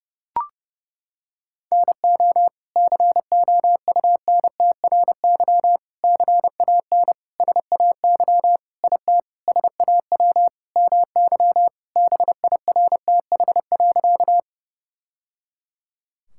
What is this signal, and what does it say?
Morse code, heard as a keyed tone, about 30 wpm, 700 Hz. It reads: NO COUNTRY CAN SAY IT SAW MY BIRTH.
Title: Sentences from Top 1000 Words - Review Part 5 - 30wpm